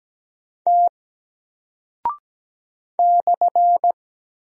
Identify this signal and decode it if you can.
Morse code, heard as a keyed tone, about 17 wpm, 700 Hz. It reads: T /
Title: Single Character (Letter:Number:Punctuation:Prosign) Alt 17wpm